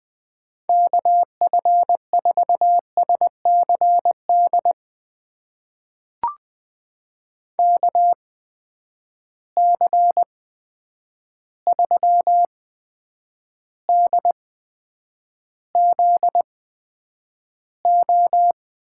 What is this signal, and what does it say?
Morse code, heard as a keyed tone, about 20 wpm, spaced as for 9 wpm, 700 Hz. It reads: KF4SCD KC3DZO
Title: USA Call Signs - 6 Characters - 8x Character Spacing - 20wpm